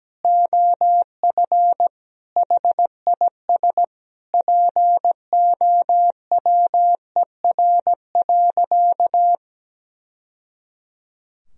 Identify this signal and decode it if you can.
Morse code, heard as a keyed tone, about 17 wpm, 700 Hz. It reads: OF HIS POWER.
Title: Sentences from Top 1000 Words - Review Part 2 - 17wpm